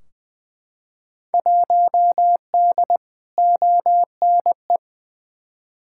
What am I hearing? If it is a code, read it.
Morse code, heard as a keyed tone, about 20 wpm, 700 Hz. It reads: 1D ONE
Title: Field Day Exchanges - 20wpm